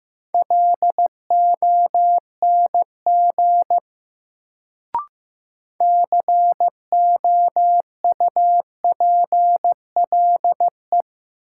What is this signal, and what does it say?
Morse code, heard as a keyed tone, about 15 wpm, 700 Hz. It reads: LONG COUPLE
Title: Sets of 2 Words - New Word Drill - 15wpm